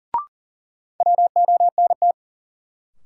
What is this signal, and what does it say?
Morse code, heard as a keyed tone, about 37 wpm, 700 Hz. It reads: WONT
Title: Sentences from Top 1000 Words - Review Part 6 - New Word Drill - 40wpm